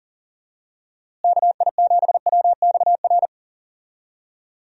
Morse code, heard as a keyed tone, about 37 wpm, 700 Hz. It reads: KI7WXR
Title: USA Call Signs - 6 Characters - Repeated 3x - 40wpm